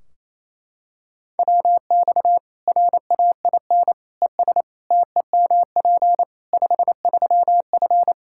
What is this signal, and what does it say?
Morse code, heard as a keyed tone, about 28 wpm, 700 Hz. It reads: WX RAIN ES TEMP 53F
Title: Practice QSO - Exchange 2 - Info - 28wpm